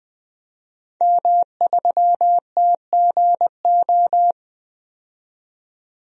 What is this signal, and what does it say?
Morse code, heard as a keyed tone, about 20 wpm, 700 Hz. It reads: M3TGO
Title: DXCC-100 Call Signs - 5 Characters - 20wpm